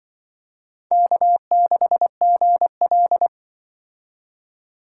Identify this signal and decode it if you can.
Morse code, heard as a keyed tone, about 25 wpm, 700 Hz. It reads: K6GL
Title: Speed Racing - USA 1x2 Call Signs 20wpm